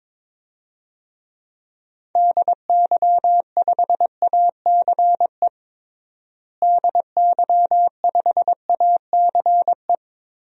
Morse code, heard as a keyed tone, about 21 wpm, 700 Hz. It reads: DY5ACE DY5ACE
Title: DXCC-100 Call Signs - 6 Characters - Repeated 2x - 22wpm